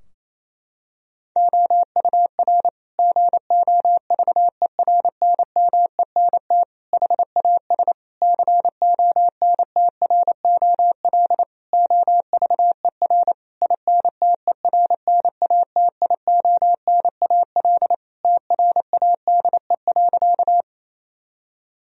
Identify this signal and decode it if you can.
Morse code, heard as a keyed tone, about 28 wpm, 700 Hz. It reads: OUR GOVERNMENT HAS CONTROL OVER INTERNATIONAL TRADE.
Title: Sentences from Top 1000 Words - Review Part 4 - 28wpm